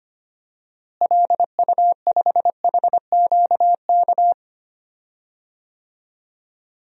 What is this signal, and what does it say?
Morse code, heard as a keyed tone, about 25 wpm, 700 Hz. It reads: LU5HQK
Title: DXCC-100 Call Signs - 6 Characters - Repeated 2x - 25wpm